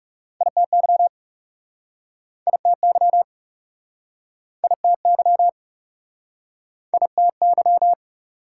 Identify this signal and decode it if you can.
Morse code, heard as a keyed tone, about 37 wpm, 700 Hz. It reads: ITY ITY ITY ITY
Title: Speed-Racing Prefixes and Suffixes 30wpm